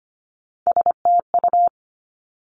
Morse code, heard as a keyed tone, about 25 wpm, 700 Hz. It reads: STU
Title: Three Challenging Letter-Number Combinations 25wpm